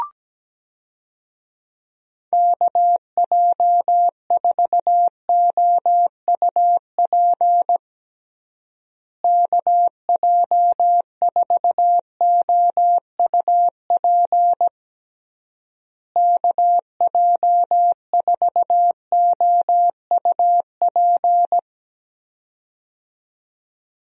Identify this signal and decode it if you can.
Morse code, heard as a keyed tone, about 17 wpm, 700 Hz. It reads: KJ4OUP KJ4OUP KJ4OUP
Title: USA Call Signs - 6 Characters - Repeated 3x - 17wpm